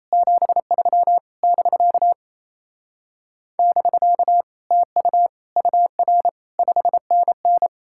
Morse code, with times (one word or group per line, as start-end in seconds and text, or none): 0.1-1.2 73
1.4-2.1 <BK>
3.6-4.4 <BK>
4.7-5.3 TU
5.6-6.3 UR
6.6-7.7 5NN